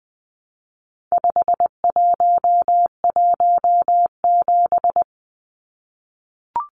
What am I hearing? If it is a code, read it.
Morse code, heard as a keyed tone, about 20 wpm, 700 Hz. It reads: 5117
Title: Numbers - 4-Digit 20wpm